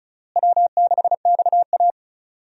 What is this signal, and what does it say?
Morse code, heard as a keyed tone, about 36 wpm, 700 Hz. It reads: W6XA
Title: USA Call Signs - 4 Characters - Repeated 3x - 35wpm